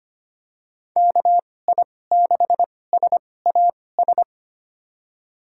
Morse code, heard as a keyed tone, about 25 wpm, spaced as for 18 wpm, 700 Hz. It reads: KI6SAS
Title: USA Call Signs - 6 Characters - 2x Character Spacing - 25wpm